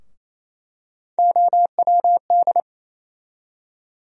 Morse code, heard as a keyed tone, about 28 wpm, 700 Hz. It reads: OWD
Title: USA Call Sign Postfixes - 3 Letters 28wpm